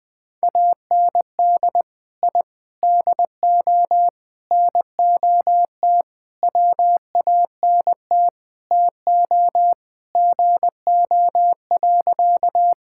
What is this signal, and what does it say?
Morse code, heard as a keyed tone, about 20 wpm, 700 Hz. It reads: AND I DO NOT WANT TO GO.
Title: Sentences with 5-Word Phrases - Spoken Beginning - 20wpm